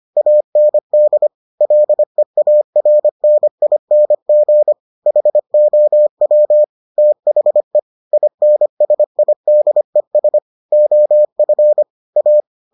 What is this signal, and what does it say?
Morse code, heard as a keyed tone, about 25 wpm, 600 Hz. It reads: AND LEARNING HOW THE INSIDES OF A